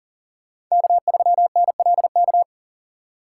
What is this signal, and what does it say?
Morse code, heard as a keyed tone, about 40 wpm, 700 Hz. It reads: K3NLK
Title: USA Call Signs - 5 Characters - 3x Character Spacing - 40wpm